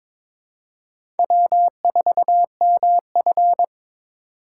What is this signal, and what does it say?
Morse code, heard as a keyed tone, about 22 wpm, 700 Hz. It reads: W4MF
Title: USA Call Signs - 4 Characters - 3x Character Spacing - 22wpm